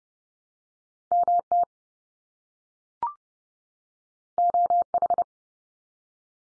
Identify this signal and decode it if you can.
Morse code, heard as a keyed tone, about 30 wpm, 700 Hz. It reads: MT OH
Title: US States 30WPM